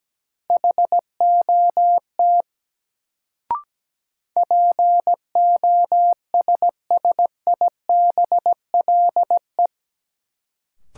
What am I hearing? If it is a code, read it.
Morse code, heard as a keyed tone, about 17 wpm, 700 Hz. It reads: HOT POSSIBLE
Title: Sentences from Top 500 Words - New Word Drill - 17wpm